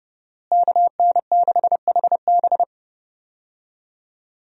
Morse code, heard as a keyed tone, about 30 wpm, 700 Hz. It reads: KN6HB
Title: USA Call Signs - 5 Characters - Repeated 3x - 30wpm